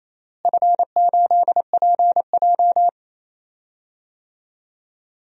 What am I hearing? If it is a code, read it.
Morse code, heard as a keyed tone, about 28 wpm, 700 Hz. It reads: F8PJ
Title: DXCC-100 Call Signs - 4 Characters - Repeated 2x - 28wpm